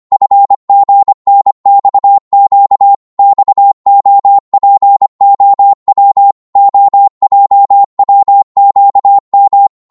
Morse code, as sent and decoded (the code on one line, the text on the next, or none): ..-. --. -. -..- --.- / -..- --- .--. --- .-- / --- .--- .-- --.- --
FGNXQ XOPOW OJWQM